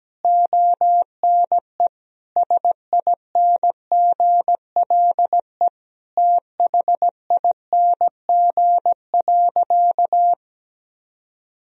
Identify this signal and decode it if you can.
Morse code, heard as a keyed tone, about 17 wpm, 700 Hz. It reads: ONE SINGLE THING.
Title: Sentences from Top 1000 Words - Review Part 4 - 17wpm